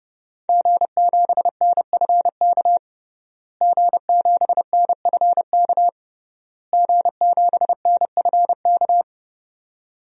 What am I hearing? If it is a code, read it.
Morse code, heard as a keyed tone, about 30 wpm, 700 Hz. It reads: G7NFK G7NFK G7NFK
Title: DXCC-100 Call Signs - 5 Characters - Repeated 3x - 30wpm